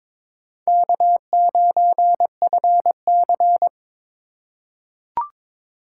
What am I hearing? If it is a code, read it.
Morse code, heard as a keyed tone, about 23 wpm, 700 Hz. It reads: K9FC
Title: USA Call Signs - 2x Character Spacing - 22wpm